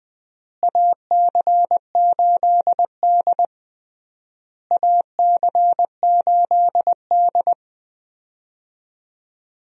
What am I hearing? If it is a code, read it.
Morse code, heard as a keyed tone, about 20 wpm, 700 Hz. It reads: AC8D AC8D
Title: USA Call Signs - 4 Characters - Repeated 3x - 20wpm